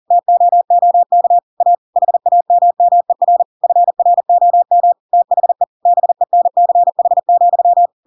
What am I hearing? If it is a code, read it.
Morse code, heard as a keyed tone, about 40 wpm, 700 Hz. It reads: TOOK A HAMMER FROM THE BENCH,
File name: Pinocchio / Pinocchio ch4 40 40 ik1hge